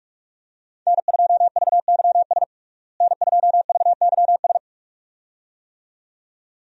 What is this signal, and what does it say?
Morse code, heard as a keyed tone, about 46 wpm, 700 Hz. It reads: N2VYS N2VYS
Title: USA Call Signs - 5 Characters - Repeated 2x - 45wpm